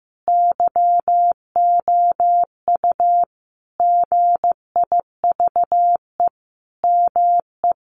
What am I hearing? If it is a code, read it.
Morse code, heard as a keyed tone, about 15 wpm, 700 Hz. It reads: YOU GIVE ME
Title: Sentences from Top 1200 Words 15wpm